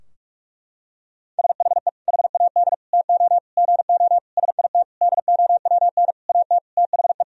Morse code, heard as a keyed tone, about 44 wpm, 700 Hz. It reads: SHE HAD TO GO SIT DOWN AT THE
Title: Sentences from Top 1000 Words - Review Part 2 - 45wpm